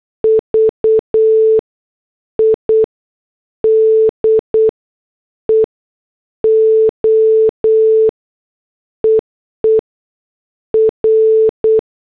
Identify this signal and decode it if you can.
Morse code, heard as a keyed tone, about 8 wpm, spaced as for 6 wpm, 440 Hz. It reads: VIDEO EE R
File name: morse / headlines